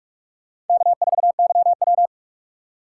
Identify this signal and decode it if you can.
Morse code, heard as a keyed tone, about 46 wpm, 700 Hz. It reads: K4YW
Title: USA Call Signs - 4 Characters - 2x Character Spacing - 45wpm